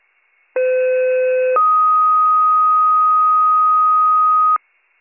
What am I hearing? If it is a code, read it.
Two-tone page, heard as a one-second tone, then a three-second tone, about 510 then 1250 Hz. It is heard from 0.6 to 4.6 s.